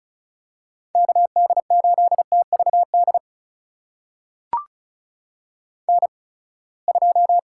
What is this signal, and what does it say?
Morse code, heard as a keyed tone, about 36 wpm, spaced as for 24 wpm, 700 Hz. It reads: KD8TVD N2
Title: USA Call Signs - 8x Character Spacing - 35wpm